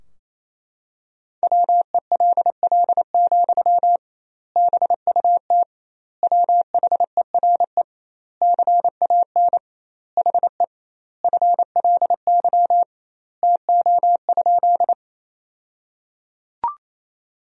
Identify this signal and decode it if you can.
Morse code, heard as a keyed tone, about 28 wpm, 700 Hz. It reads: WELL, BUT WHERE CAN HE FLY TO?
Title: 2x Word-Spacing - Sentences from Top 500 Words - 28wpm